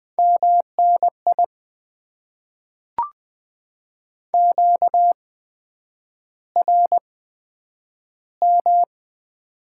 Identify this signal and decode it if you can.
Morse code, heard as a keyed tone, about 20 wpm, spaced as for 9 wpm, 700 Hz. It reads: MNI QRM
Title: Top 90 QSO Elements - 8x Character Spacing - 20wpm